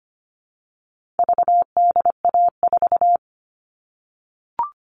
Morse code, heard as a keyed tone, about 25 wpm, 700 Hz. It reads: VDA4
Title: All Cut Numbers - 4-Digit 25wpm